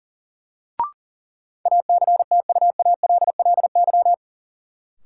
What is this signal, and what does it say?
Morse code, heard as a keyed tone, about 40 wpm, 700 Hz. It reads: ACTUALLY
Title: Sentences from Top 700 Words - New Word Drill - 40wpm